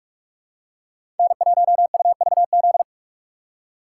Morse code, heard as a keyed tone, about 47 wpm, 700 Hz. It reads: N1UVZ
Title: USA Call Signs - 5 Characters - 6x Character Spacing - 45wpm